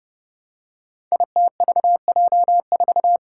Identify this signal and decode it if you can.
Morse code, heard as a keyed tone, about 30 wpm, 700 Hz. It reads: ITVJ4
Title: Five Challenging Letter-Number Combinations 30wpm